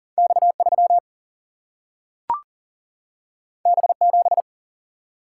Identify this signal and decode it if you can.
Morse code, heard as a keyed tone, about 40 wpm, 700 Hz. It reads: X3 B7
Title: Single-Letter Single-Number 40wpm